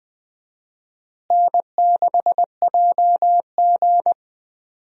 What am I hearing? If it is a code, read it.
Morse code, heard as a keyed tone, about 20 wpm, 700 Hz. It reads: N6JG
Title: USA Call Signs - 4 Characters - Repeated 2x - 20wpm